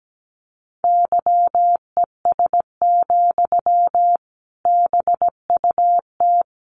Morse code, heard as a keyed tone, about 17 wpm, 700 Hz. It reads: YES, BUT
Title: Sentences from Top 500 Words 17wpm